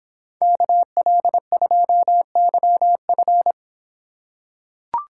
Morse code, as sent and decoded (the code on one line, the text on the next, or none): -.- .-.. ..--- -.-- ..-.
KL2YF